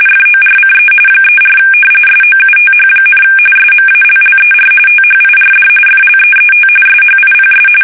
RTTY, about 45.45 baud. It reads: FORCE 34 SOUTH, SMALLER THIRD FLEET SHIPS
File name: fsk850 / battleshipnj